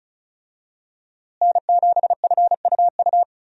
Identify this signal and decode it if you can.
Morse code, heard as a keyed tone, about 34 wpm, 700 Hz. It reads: N7FUU
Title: USA Call Signs - 5 Characters - Repeated 3x - 35wpm